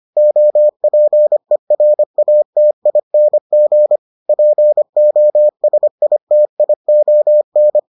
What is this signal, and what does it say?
Morse code, heard as a keyed tone, about 25 wpm, 600 Hz. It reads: OPERATING POSITION